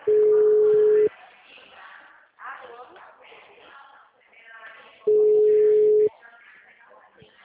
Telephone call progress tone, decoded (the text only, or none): ringback tone